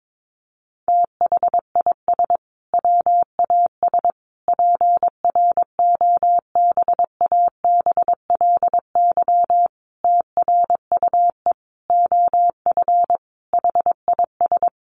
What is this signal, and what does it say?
Morse code, heard as a keyed tone, about 22 wpm, 700 Hz. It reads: THIS WAS PROBABLY TRUE OF HIS
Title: Sentences from Top 700 Words 22wpm